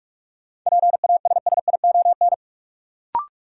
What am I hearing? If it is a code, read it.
Morse code, heard as a keyed tone, about 43 wpm, 700 Hz. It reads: PASSION
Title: Sentences from Top 1300 Words - New Word Drill - 45wpm